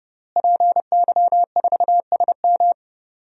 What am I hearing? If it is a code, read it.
Morse code, heard as a keyed tone, about 31 wpm, 700 Hz. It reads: PY4SM
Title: DXCC-100 Call Signs - 5 Characters - Repeated 3x - 30wpm